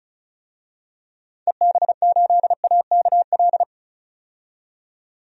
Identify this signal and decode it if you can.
Morse code, heard as a keyed tone, about 35 wpm, 700 Hz. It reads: EB8AKL
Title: DXCC-100 Call Signs - 6 Characters - 35wpm